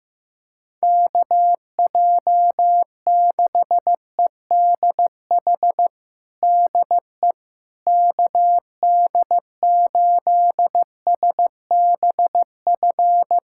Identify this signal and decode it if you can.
Morse code, heard as a keyed tone, about 16 wpm, 700 Hz. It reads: KJ6EDH DE KD8SBF
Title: USA Callsign DE Callsign - 6 Characters - 15wpm